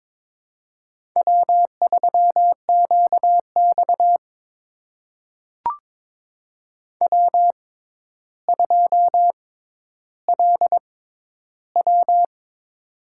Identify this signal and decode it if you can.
Morse code, heard as a keyed tone, about 22 wpm, spaced as for 11 wpm, 700 Hz. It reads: W3QX W2LW
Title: USA Call Signs - 4 Characters - 6x Character Spacing - 22wpm